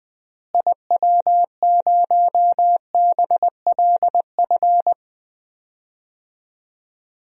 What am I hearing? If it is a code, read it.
Morse code, heard as a keyed tone, about 21 wpm, 700 Hz. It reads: IW0BLF
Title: DXCC-100 Call Signs - 6 Characters - Repeated 3x - 20wpm